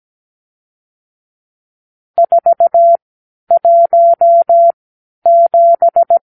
Morse code, heard as a keyed tone, about 17 wpm, spaced as for 10 wpm, 680 Hz. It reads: 417
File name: arrl / Review (1-0)